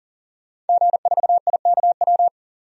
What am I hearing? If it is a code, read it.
Morse code, heard as a keyed tone, about 40 wpm, 700 Hz. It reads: G4IKW
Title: DXCC-100 Call Signs - 5 Characters - 40wpm